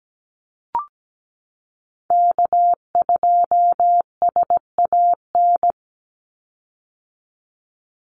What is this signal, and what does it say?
Morse code, heard as a keyed tone, about 17 wpm, 700 Hz. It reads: K2SAN
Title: Call Signs - 17wpm